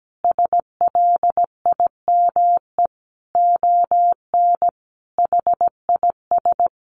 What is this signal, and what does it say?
Morse code, heard as a keyed tone, about 17 wpm, 700 Hz. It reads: SLIME ON HIS
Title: A Sound Of Thunder 17wpm